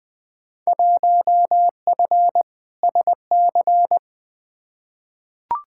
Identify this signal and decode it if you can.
Morse code, heard as a keyed tone, about 20 wpm, 700 Hz. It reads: 1F SC
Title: Field Day Exchanges Encore - 20wpm